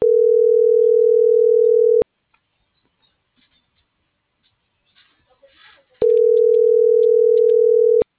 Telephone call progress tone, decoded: ringback tone

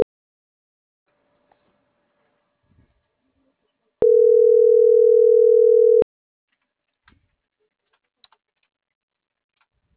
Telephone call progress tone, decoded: ringback tone

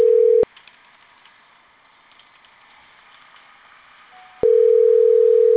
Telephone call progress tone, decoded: ringback tone

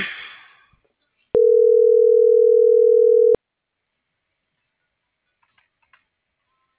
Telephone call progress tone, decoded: ringback tone

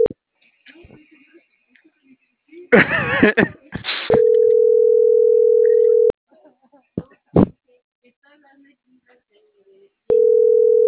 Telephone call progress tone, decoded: ringback tone

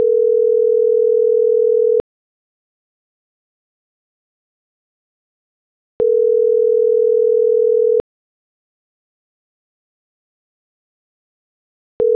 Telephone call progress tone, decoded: ringback tone

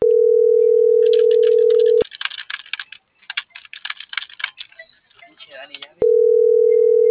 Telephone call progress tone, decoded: ringback tone

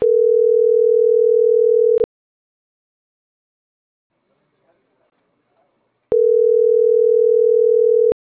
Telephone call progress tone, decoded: ringback tone

